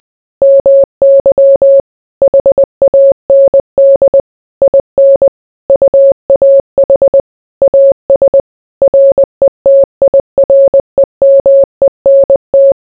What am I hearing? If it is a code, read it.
Morse code, heard as a keyed tone, about 20 wpm, 550 Hz. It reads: MY HAND IN UAH AS RETIREMENT